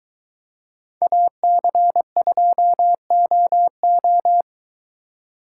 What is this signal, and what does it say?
Morse code, heard as a keyed tone, about 22 wpm, 700 Hz. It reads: AC2OO